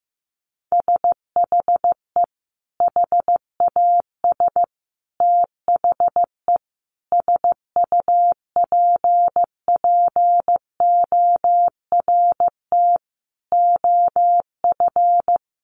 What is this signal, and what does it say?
Morse code, heard as a keyed tone, about 15 wpm, 700 Hz. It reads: SHE HAS THE SUPPORT OF
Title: Sentences from Top 800 Words 15wpm